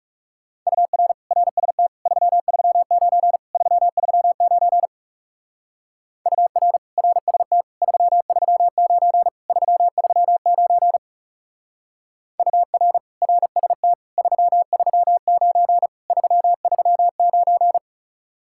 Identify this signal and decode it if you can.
Morse code, heard as a keyed tone, about 35 wpm, 700 Hz. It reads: UR RST 339 339 UR RST 339 339 UR RST 339 339
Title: Speed-Racing Common QSO Phrases - 30wpm